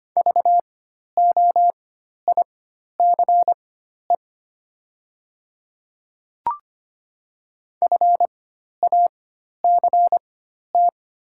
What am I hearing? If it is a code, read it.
Morse code, heard as a keyed tone, about 25 wpm, spaced as for 12 wpm, 700 Hz. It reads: VOICE FACT
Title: 4x Character Spacing - Top 500 Words - 25wpm